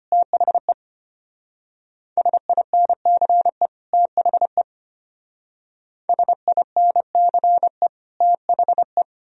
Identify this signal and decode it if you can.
Morse code, heard as a keyed tone, about 33 wpm, 700 Hz. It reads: THE SINCE THE SINCE THE